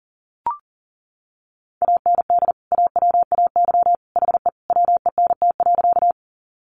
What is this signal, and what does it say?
Morse code, heard as a keyed tone, about 40 wpm, 700 Hz. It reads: AND AWAY HE WENT.
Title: Sentences from Top 600 Words 40wpm